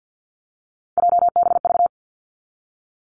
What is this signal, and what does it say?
Morse code, heard as a keyed tone, about 50 wpm, 700 Hz. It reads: JBV
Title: Three Challenging Letter-Number Combinations 50wpm